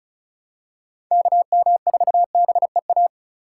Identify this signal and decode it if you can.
Morse code, heard as a keyed tone, about 34 wpm, 690 Hz. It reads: KM4BEA
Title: USA Call Signs - 6 Characters - 4x Character Spacing - 35wpm